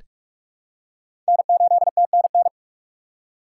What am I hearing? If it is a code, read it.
Morse code, heard as a keyed tone, about 46 wpm, 700 Hz. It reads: N8TNN
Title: USA Call Signs - 5 Characters - 2x Character Spacing - 45wpm